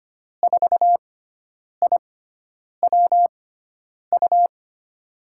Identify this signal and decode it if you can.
Morse code, heard as a keyed tone, about 25 wpm, spaced as for 9 wpm, 700 Hz. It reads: 4IWU
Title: USA Call Signs - 6 Characters - 6x Character Spacing - 25wpm